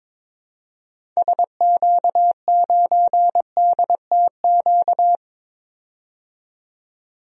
Morse code, heard as a keyed tone, about 22 wpm, 700 Hz. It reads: SQ9DTQ
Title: DXCC-100 Call Signs - 6 Characters - 22wpm